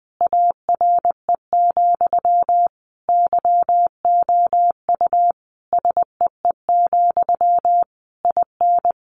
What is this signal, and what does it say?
Morse code, heard as a keyed tone, about 20 wpm, 700 Hz. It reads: ARE, YOU SEE, IN
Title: Sentences from Top 1200 Words 20wpm